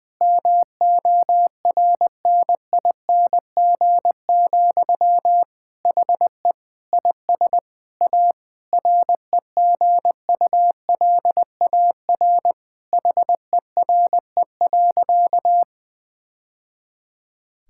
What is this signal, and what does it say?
Morse code, heard as a keyed tone, about 20 wpm, 700 Hz. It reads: MORNING, HE IS A REGULAR HERE.
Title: Sentences from Top 1000 Words - Review Part 6 - 20wpm